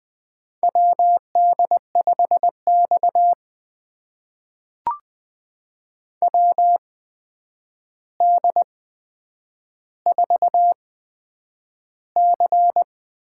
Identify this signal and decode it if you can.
Morse code, heard as a keyed tone, about 20 wpm, spaced as for 9 wpm, 700 Hz. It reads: WD5X WD4C
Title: USA Call Signs - 4 Characters - 8x Character Spacing - 20wpm